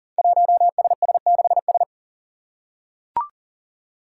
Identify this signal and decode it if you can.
Morse code, heard as a keyed tone, about 39 wpm, 700 Hz. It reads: 1SS6S
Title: Five Challenging Letter-Number Combinations 40wpm